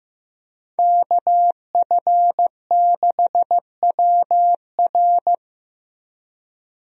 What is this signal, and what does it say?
Morse code, heard as a keyed tone, about 15 wpm, 700 Hz. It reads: KF6WR
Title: USA Call Signs - 5 Characters - Repeated 2x - 15wpm